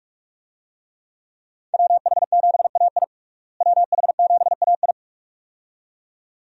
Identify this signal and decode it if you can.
Morse code, heard as a keyed tone, about 46 wpm, 700 Hz. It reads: WH7AI WH7AI
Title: USA Call Signs - 5 Characters - Repeated 2x - 45wpm